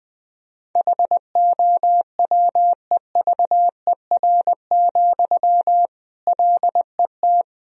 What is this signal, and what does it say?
Morse code, heard as a keyed tone, about 20 wpm, 700 Hz. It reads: HOWEVER, LET
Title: Sentences from Top 1000 Words - Review Part 1 - 20wpm